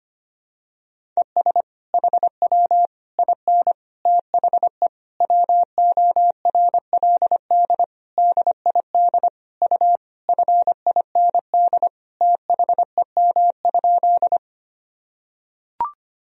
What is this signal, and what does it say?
Morse code, heard as a keyed tone, about 25 wpm, 700 Hz. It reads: ES HW IN THE WORLD DID U FIND THEM?
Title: Sentences from Top 200 Words and Abbreviations 25wpm